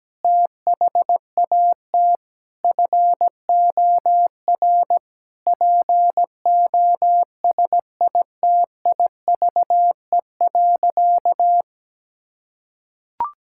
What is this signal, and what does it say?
Morse code, heard as a keyed tone, about 17 wpm, 700 Hz. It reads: THAT FOR POSITIVE.
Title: Sentences from Top 1000 Words - Review Part 5 - 17wpm